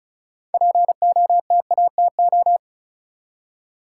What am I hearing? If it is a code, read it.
Morse code, heard as a keyed tone, about 36 wpm, 700 Hz. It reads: POTATO
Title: Six Letter Words - 35wpm